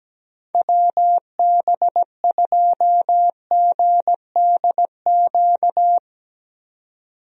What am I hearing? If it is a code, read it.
Morse code, heard as a keyed tone, about 17 wpm, 700 Hz. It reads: WB2GDQ
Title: USA Call Signs - 6 Characters - 17wpm